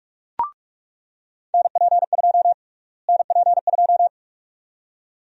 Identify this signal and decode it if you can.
Morse code, heard as a keyed tone, about 46 wpm, 700 Hz. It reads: NP2 NP2
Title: USA Call Sign Prefixes - Repeated 2x - 45wpm